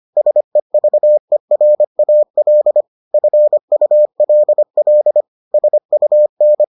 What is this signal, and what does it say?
Morse code, heard as a keyed tone, about 25 wpm, 600 Hz. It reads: SEVERAL FULL SUN